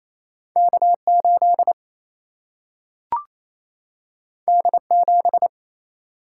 Morse code, heard as a keyed tone, about 28 wpm, 700 Hz. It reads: K 8 D 7
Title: Single-Letter Single-Number 28wpm